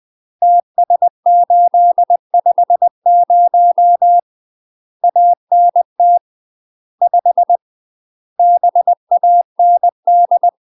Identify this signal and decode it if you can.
Morse code, heard as a keyed tone, about 20 wpm, 700 Hz. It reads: TS850 ANT 5 BAND
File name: UK MASTER QSOs TRAINING / 20WPM / UK MASTER QSOs 0020